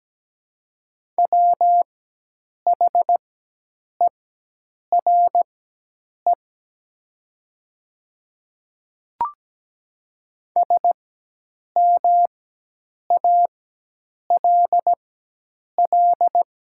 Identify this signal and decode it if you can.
Morse code, heard as a keyed tone, about 17 wpm, spaced as for 8 wpm, 700 Hz. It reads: WHERE SMALL
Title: 4x Character Spacing - Top 300 Words - 17wpm